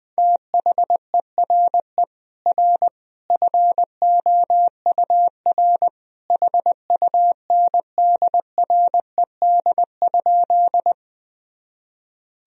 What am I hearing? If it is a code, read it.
Morse code, heard as a keyed tone, about 21 wpm, 700 Hz. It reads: THERE R FOUR HUNDRED?
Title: Sentences from Top 400 Words and Abbreviations 20wpm